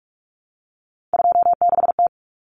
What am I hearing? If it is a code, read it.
Morse code, heard as a keyed tone, about 44 wpm, 700 Hz. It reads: 26T
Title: Common Cut Numbers - 3-Digit 45wpm